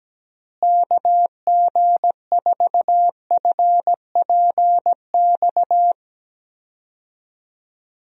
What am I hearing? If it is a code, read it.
Morse code, heard as a keyed tone, about 17 wpm, 700 Hz. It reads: KG4FPX
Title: USA Call Signs - 6 Characters - Repeated 3x - 17wpm